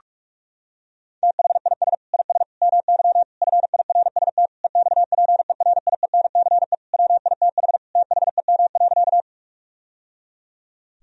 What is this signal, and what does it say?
Morse code, heard as a keyed tone, about 46 wpm, 700 Hz. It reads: THIS IS MY FIRST EXPERIENCE WITH THEM.
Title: Sentences from Top 1000 Words - Review Part 3 - 45wpm